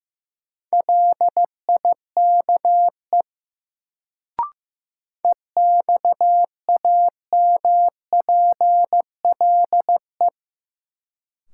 Morse code, heard as a keyed tone, about 15 wpm, 700 Hz. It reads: LIKE EXAMPLE
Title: Sets of 2 Words - New Word Drill - 15wpm